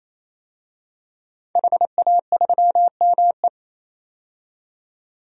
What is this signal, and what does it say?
Morse code, heard as a keyed tone, about 27 wpm, 700 Hz. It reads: HA3ME
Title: DXCC-100 Call Signs - 5 Characters - 28wpm